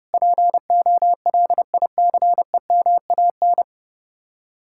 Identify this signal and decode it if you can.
Morse code, heard as a keyed tone, about 37 wpm, 700 Hz. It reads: POLICEMAN
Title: Speed-Racing Common QSO Phrases Encore - 30wpm